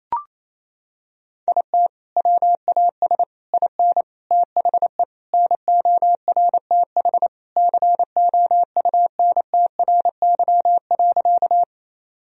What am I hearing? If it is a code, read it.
Morse code, heard as a keyed tone, about 28 wpm, 700 Hz. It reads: IT WAS IN THE NORTH COUNTRY.
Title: Sentences from Top 1000 Words - Review Part 1 - 28wpm